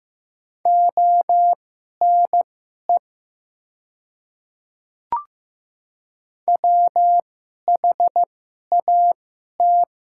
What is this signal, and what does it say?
Morse code, heard as a keyed tone, about 15 wpm, spaced as for 11 wpm, 700 Hz. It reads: ONE WHAT
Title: 2x Character Spacing - Top 100 Words - 15wpm